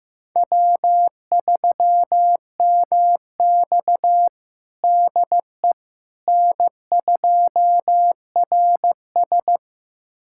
Morse code, heard as a keyed tone, about 16 wpm, 700 Hz. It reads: W3MX DE N2RS
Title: USA Callsign DE Callsign - 4 Characters - 15wpm